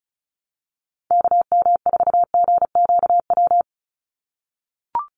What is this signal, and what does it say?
Morse code, heard as a keyed tone, about 34 wpm, 700 Hz. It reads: KM4GQW
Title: USA Call Signs 35wpm